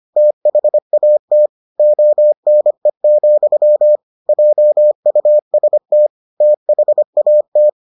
Morse code, heard as a keyed tone, about 25 wpm, 600 Hz. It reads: THAT ONE, JUST THAT